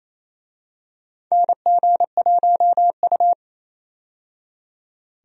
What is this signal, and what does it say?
Morse code, heard as a keyed tone, about 28 wpm, 700 Hz. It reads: NG1U
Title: USA Call Signs - 4 Characters - 28wpm